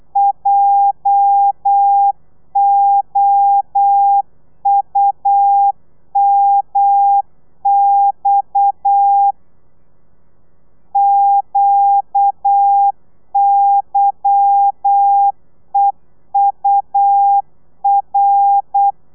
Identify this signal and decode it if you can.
Morse code, heard as a keyed tone, about 8 wpm, 800 Hz. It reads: JOUMX QYEUR
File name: Lez11